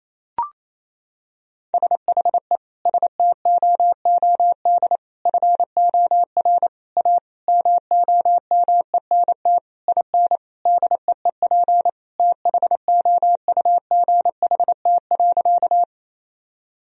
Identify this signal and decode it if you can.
Morse code, heard as a keyed tone, about 28 wpm, 700 Hz. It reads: SHE STOOD FOR A MOMENT IN DEEP THOUGHT.
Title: Sentences from Top 1000 Words - Review Part 3 - 28wpm